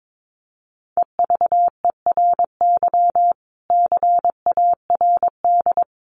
Morse code, heard as a keyed tone, about 22 wpm, 700 Hz. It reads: EVERY CARD
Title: Sentences from Top 1400 Words 22wpm